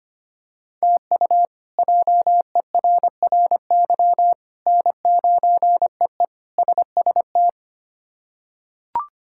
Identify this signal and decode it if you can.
Morse code, heard as a keyed tone, about 25 wpm, 700 Hz. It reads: TU JERRY N9EE SST
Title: K1USN SST QSO Practice - 25wpm